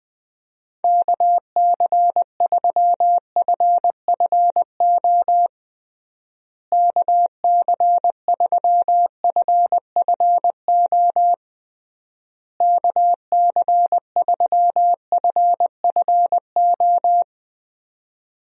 Morse code, heard as a keyed tone, about 20 wpm, 700 Hz. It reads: KC3FFO KC3FFO KC3FFO
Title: USA Call Signs - 6 Characters - Repeated 3x - 20wpm